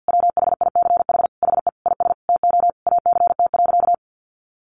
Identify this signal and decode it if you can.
Morse code, heard as a keyed tone, about 50 wpm, 700 Hz. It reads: WHICH HE IS TO ACT.
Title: Sentences from Top 1200 Words 50wpm